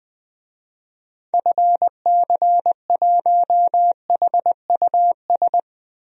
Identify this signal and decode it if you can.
Morse code, heard as a keyed tone, about 20 wpm, 700 Hz. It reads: FC1HUS
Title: DXCC-100 Call Signs - 6 Characters - 20wpm